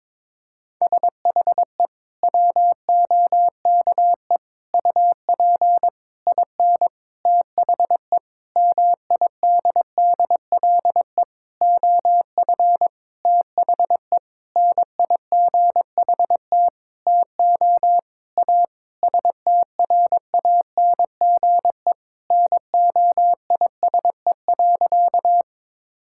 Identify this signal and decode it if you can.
Morse code, heard as a keyed tone, about 22 wpm, 700 Hz. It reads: SHE WOKE UP IN THE MIDDLE OF THE NIGHT TO A STRANGE NOISE.
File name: practice / 22wpm / Sentences with 8-Word Phrases - 22wpm